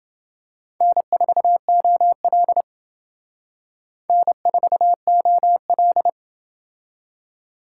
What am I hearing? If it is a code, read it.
Morse code, heard as a keyed tone, about 28 wpm, 700 Hz. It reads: N4OL N4OL